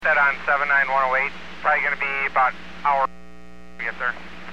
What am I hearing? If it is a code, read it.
Emergency Alert System (SAME) tones (header bursts), heard as harsh buzzing bursts, with speech. It is heard from 0.0 to 4.2 s.